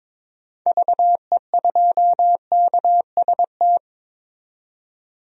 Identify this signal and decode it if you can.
Morse code, heard as a keyed tone, about 22 wpm, 700 Hz. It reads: VE2KST
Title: DXCC-100 Call Signs - 6 Characters - 22wpm